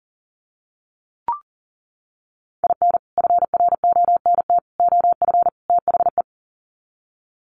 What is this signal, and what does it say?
Morse code, heard as a keyed tone, about 40 wpm, 700 Hz. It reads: IN FRONT OF THE
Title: Sets of 4 Words 40wpm